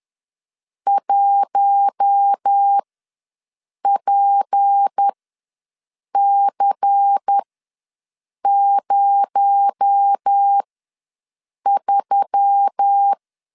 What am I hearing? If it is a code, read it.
Morse code, heard as a keyed tone, about 10 wpm, spaced as for 6 wpm, 800 Hz. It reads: 1PC03
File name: LEZIONI / 7 GF9Z / 7 GF9Z ws 10wpm 3919